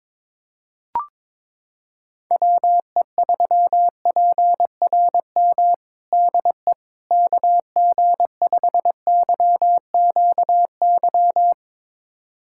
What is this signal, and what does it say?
Morse code, heard as a keyed tone, about 21 wpm, 700 Hz. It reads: WE3PRM DE KG5YQY
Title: USA Callsign DE Callsign - 6 Characters - 22wpm